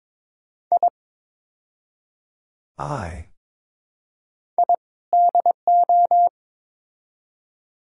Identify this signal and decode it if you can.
Morse code, heard as a keyed tone, about 21 wpm, 700 Hz. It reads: I I DO
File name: practice / 22wpm / Build A Phrase - 5-Word Phrases - 22wpm